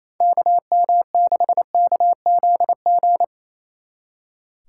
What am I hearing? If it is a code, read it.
Morse code, heard as a keyed tone, about 28 wpm, 700 Hz. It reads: KM6KZG
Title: Speed Racing - USA 2x3 Call Signs 28wpm